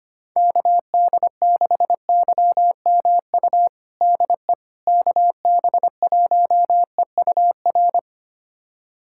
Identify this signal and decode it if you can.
Morse code, heard as a keyed tone, about 25 wpm, 700 Hz. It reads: KD6YMU DE KB1EUR
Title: USA Callsign DE Callsign - 6 Characters - 25wpm